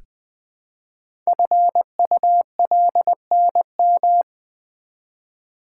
Morse code, heard as a keyed tone, about 20 wpm, 700 Hz. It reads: FULNM
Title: Code Groups - 5 Letters - 20wpm